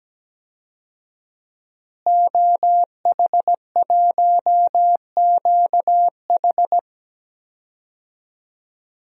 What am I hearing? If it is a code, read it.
Morse code, heard as a keyed tone, about 17 wpm, 700 Hz. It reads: OH1QH
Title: DXCC-100 Call Signs - 5 Characters - Repeated 3x - 17wpm